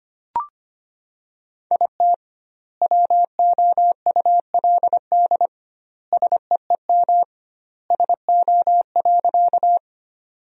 Morse code, heard as a keyed tone, about 25 wpm, 700 Hz. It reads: IT WOULD SEEM SO.
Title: 2x Word-Spacing - Sentences from Top 300 Words - 25wpm